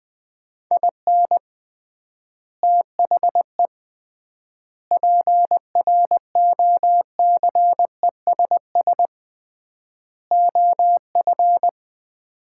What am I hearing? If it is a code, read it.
Morse code, heard as a keyed tone, about 20 wpm, 700 Hz. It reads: IN THE PROCESS OF
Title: 3x Word-Spacing - Sets of 4 Words - 20wpm